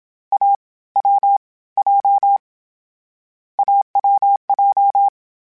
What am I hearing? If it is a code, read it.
Morse code, heard as a keyed tone, about 27 wpm, spaced as for 19 wpm, 800 Hz. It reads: AWJ AWJ
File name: Difficult Combos / Difficult Combos-mp3s / AWJ ShortRun@25